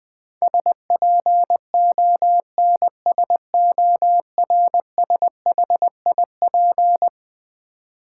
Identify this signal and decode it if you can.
Morse code, heard as a keyed tone, about 20 wpm, 700 Hz. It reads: SPONSORSHIP
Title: Build A Word - 4 Letters At A Time - 11-Letter Words - 20wpm